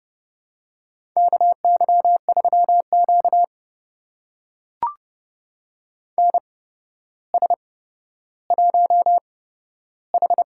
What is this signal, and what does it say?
Morse code, heard as a keyed tone, about 30 wpm, spaced as for 13 wpm, 700 Hz. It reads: KY3Q NS1H
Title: USA Call Signs - 4 Characters - 8x Character Spacing - 30wpm